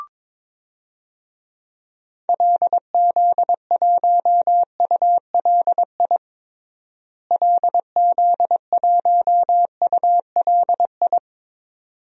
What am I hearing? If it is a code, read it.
Morse code, heard as a keyed tone, about 22 wpm, 700 Hz. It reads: LZ1ULI LZ1ULI
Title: DXCC-100 Call Signs - 6 Characters - Repeated 2x - 22wpm